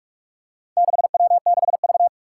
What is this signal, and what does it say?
Morse code, heard as a keyed tone, about 44 wpm, 700 Hz. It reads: 6W6V
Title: Four Challenging Letter-Number Combinations 45wpm